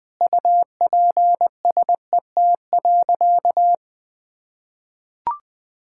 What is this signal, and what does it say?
Morse code, heard as a keyed tone, about 20 wpm, 700 Hz. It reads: UPSET.
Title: Sentences from Top 1000 Words - Review Part 5 - 20wpm